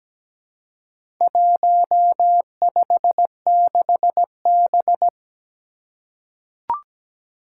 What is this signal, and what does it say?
Morse code, heard as a keyed tone, about 17 wpm, 700 Hz. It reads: 156B
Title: Four Challenging Letter-Number Combinations 17wpm